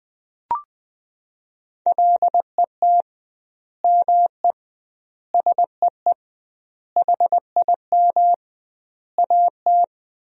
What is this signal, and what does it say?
Morse code, heard as a keyed tone, about 21 wpm, 700 Hz. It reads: LET ME SEE HIM AT
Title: 2x Word-Spacing - Sentences from Top 300 Words - 20wpm